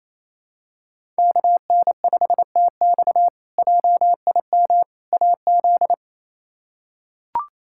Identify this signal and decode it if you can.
Morse code, heard as a keyed tone, about 28 wpm, 700 Hz. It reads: KN5TX JIM AZ
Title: K1USN SST QSO Practice - 28wpm